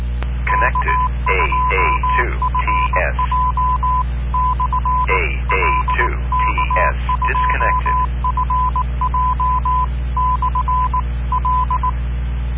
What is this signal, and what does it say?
Morse code, heard as a keyed tone, about 18 wpm, 1010 Hz. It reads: K2FJ/L K2FJ/L